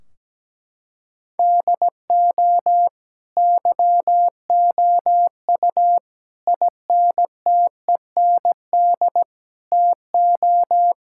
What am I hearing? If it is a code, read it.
Morse code, heard as a keyed tone, about 17 wpm, 700 Hz. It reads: DO YOU INTEND TO
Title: Sentences from Top 1600 Words 17wpm